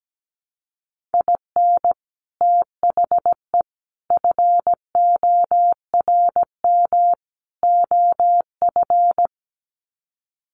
Morse code, heard as a keyed tone, about 17 wpm, 700 Hz. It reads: IN THE FORM OF
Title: Sets of 4 Words 17wpm